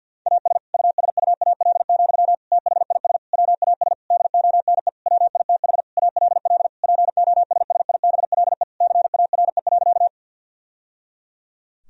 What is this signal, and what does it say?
Morse code, heard as a keyed tone, about 51 wpm, 700 Hz. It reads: AS USUAL, THIS WAS DONE WITH ALL POSSIBLE CARE.
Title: Sentences from Top 1000 Words - Review Part 2 - 50wpm